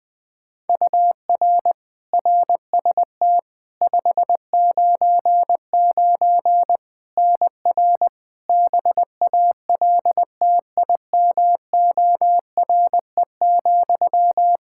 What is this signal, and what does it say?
Morse code, heard as a keyed tone, about 20 wpm, 700 Hz. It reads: UR RST 599 NR BALTIMORE,
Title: Practice QSO - Exchange 1 - Beginning - 20wpm